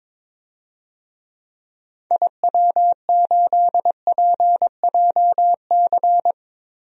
Morse code, heard as a keyed tone, about 22 wpm, 700 Hz. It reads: IW8PJC
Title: DXCC-100 Call Signs - 6 Characters - Repeated 2x - 22wpm